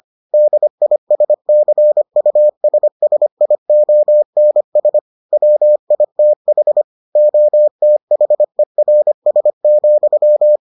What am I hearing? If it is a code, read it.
Morse code, heard as a keyed tone, about 25 wpm, 600 Hz. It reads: DISCUSSIONS WITH OTHERS,